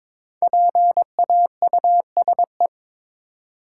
Morse code, heard as a keyed tone, about 22 wpm, 690 Hz. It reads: PAUSE
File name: 5 Letter Words - First 3 Letters Given - 22wpm